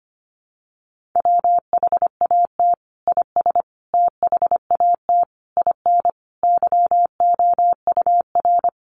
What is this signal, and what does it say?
Morse code, heard as a keyed tone, about 25 wpm, 700 Hz. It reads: WHAT IS THAT IN YOUR
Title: Sentences from Top 200 Words 25wpm